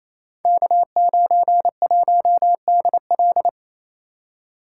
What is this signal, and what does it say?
Morse code, heard as a keyed tone, about 28 wpm, 700 Hz. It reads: K91DL
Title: Code Groups - 5 Letter:Numbers - 28wpm